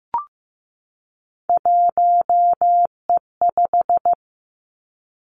Morse code, heard as a keyed tone, about 15 wpm, 700 Hz. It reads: 1E5
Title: Three Challenging Letter-Number Combinations 15wpm